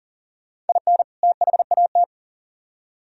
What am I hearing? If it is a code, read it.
Morse code, heard as a keyed tone, about 47 wpm, 700 Hz. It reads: IN THAT
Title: Speed-Racing - Sets of 2 Words 40wpm